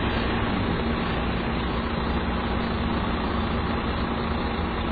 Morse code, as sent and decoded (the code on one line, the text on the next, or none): none